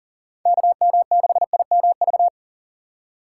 Morse code, heard as a keyed tone, about 40 wpm, 700 Hz. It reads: KM6IMV
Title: USA Call Signs - 6 Characters - 2x Character Spacing - 40wpm